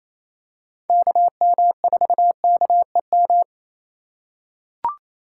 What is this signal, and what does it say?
Morse code, heard as a keyed tone, about 28 wpm, 700 Hz. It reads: KM4KEM
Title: USA Call Signs - 3x Character Spacing - 28wpm